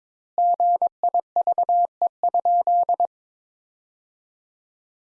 Morse code, heard as a keyed tone, about 22 wpm, 700 Hz. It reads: GIVE?
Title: Sentences Top 200 Words - 22wpm